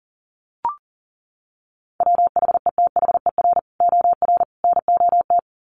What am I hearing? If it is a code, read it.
Morse code, heard as a keyed tone, about 40 wpm, 700 Hz. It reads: WHETHER OR NOT
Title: Sets of 3 Words 40wpm